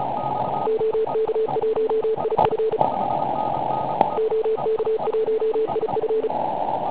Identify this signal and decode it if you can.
Morse code, heard as a keyed tone, about 34 wpm, 440 Hz, with noise. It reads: OK1IF OK1IF
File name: Pile UP CUC204